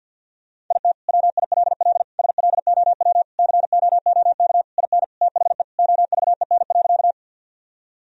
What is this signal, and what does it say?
Morse code, heard as a keyed tone, about 50 wpm, 700 Hz. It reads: IT WILL SLOW COOK IN THE OVEN.
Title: Sentences from Top 1000 Words - Review Part 4 - 50wpm